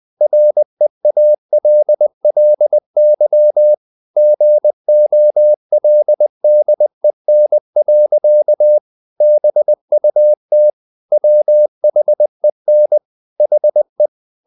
Morse code, heard as a keyed tone, about 20 wpm, 600 Hz. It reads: REALLY GOLDEN. BUT WHEN HE
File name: the old man and the sea part 2 20wpm